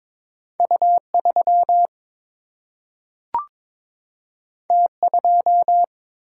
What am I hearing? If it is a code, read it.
Morse code, heard as a keyed tone, about 22 wpm, 700 Hz. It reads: U 3 T 2
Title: Single-Letter Single-Number 22wpm